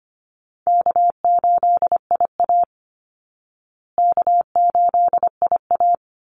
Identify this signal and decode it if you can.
Morse code, heard as a keyed tone, about 25 wpm, 700 Hz. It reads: K8IA K8IA
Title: Easy Call Signs Repeated 2x - 25wpm